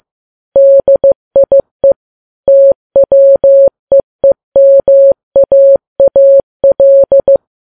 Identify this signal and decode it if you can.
Morse code, heard as a keyed tone, about 15 wpm, 550 Hz. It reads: DIE TWEEMAAL
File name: Files Web tls / Morse on9fv / morse15kps